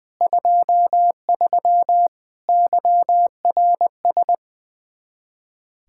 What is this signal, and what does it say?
Morse code, heard as a keyed tone, about 20 wpm, 700 Hz. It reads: 23 YRS
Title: Practice QSO - Exchange 3 - More Info - 20wpm